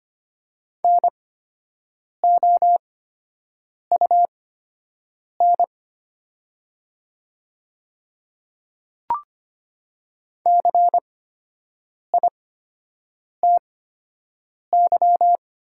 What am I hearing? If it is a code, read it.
Morse code, heard as a keyed tone, about 25 wpm, spaced as for 7 wpm, 700 Hz. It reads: NOUN CITY
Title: 8x Character Spacing - Top 500 Words - 25wpm